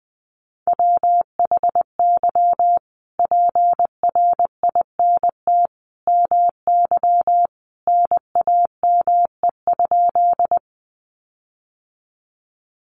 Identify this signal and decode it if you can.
Morse code, heard as a keyed tone, about 20 wpm, 700 Hz. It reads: WHY PRINT MY NAME?
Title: Sentences from Top 1400 Words 20wpm